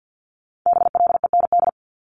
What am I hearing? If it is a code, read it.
Morse code, heard as a keyed tone, about 50 wpm, 700 Hz. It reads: BLEND
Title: Multi-letter Phonemes 50wpm